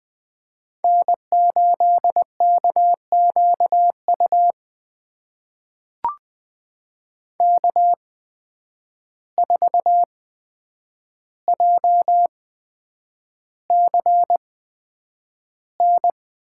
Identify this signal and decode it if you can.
Morse code, heard as a keyed tone, about 20 wpm, spaced as for 9 wpm, 700 Hz. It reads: N8KQU K4JCN
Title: USA Call Signs - 5 Characters - 8x Character Spacing - 20wpm